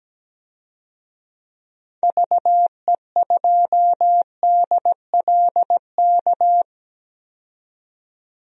Morse code, heard as a keyed tone, about 17 wpm, 700 Hz. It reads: VE2DLK
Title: DXCC-100 Call Signs - 6 Characters - 17wpm